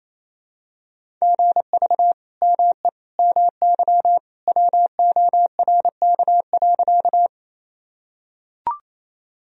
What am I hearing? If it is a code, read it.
Morse code, heard as a keyed tone, about 27 wpm, 700 Hz. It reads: GV ME MY WORK.
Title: Sentences from Top 200 Words and Abbreviations 28wpm